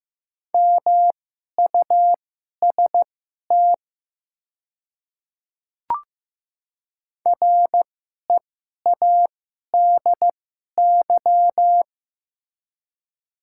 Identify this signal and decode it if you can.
Morse code, heard as a keyed tone, about 15 wpm, spaced as for 11 wpm, 700 Hz. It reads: MUST READY
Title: 2x Character Spacing - Top 500 Words - 15wpm